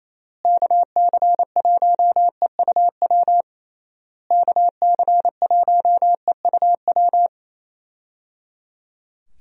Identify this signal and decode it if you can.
Morse code, heard as a keyed tone, about 27 wpm, 700 Hz. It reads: KC1EUW KC1EUW
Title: USA Call Signs - 6 Characters - Repeated 3x - 28wpm